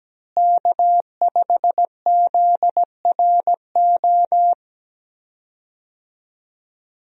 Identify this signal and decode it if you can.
Morse code, heard as a keyed tone, about 17 wpm, 700 Hz. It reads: K5ZRO
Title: USA Call Signs - 5 Characters - Repeated 3x - 17wpm